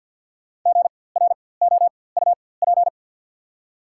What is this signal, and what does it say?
Morse code, heard as a keyed tone, about 52 wpm, spaced as for 23 wpm, 690 Hz. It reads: GROUP
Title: 4x Character Spacing - Top 500 Words - 50wpm